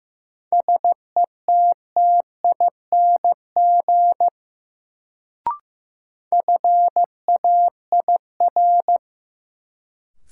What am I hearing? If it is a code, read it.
Morse code, heard as a keyed tone, about 15 wpm, 700 Hz. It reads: SETTING FAIR
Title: Sentences from Top 1300 Words - New Word Drill Alt - 15wpm